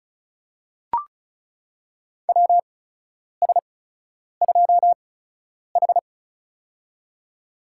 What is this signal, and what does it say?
Morse code, heard as a keyed tone, about 36 wpm, spaced as for 10 wpm, 700 Hz. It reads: WS2H
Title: USA Call Signs - 8x Character Spacing - 35wpm